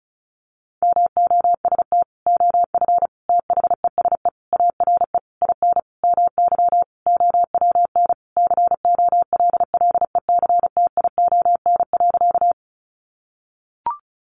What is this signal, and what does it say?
Morse code, heard as a keyed tone, about 34 wpm, 700 Hz. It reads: MOST OF THESE ARE IN MY OWN COLLECTION.
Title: Sentences from Top 1200 Words 35wpm